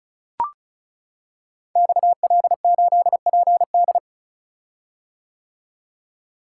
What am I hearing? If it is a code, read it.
Morse code, heard as a keyed tone, about 36 wpm, 700 Hz. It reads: XL8PD
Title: Code Groups - 5 Letter:Numbers - 35wpm